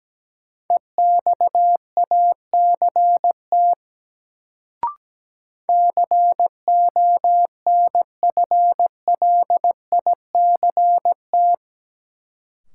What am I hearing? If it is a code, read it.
Morse code, heard as a keyed tone, about 17 wpm, 700 Hz. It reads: EXACT CONFLICT
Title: Sentences from Top 1300 Words - New Word Drill Alt - 17wpm